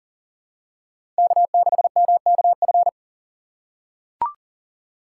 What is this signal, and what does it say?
Morse code, heard as a keyed tone, about 40 wpm, 700 Hz. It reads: K6MKF
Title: K1USN SST QSO Practice - 40wpm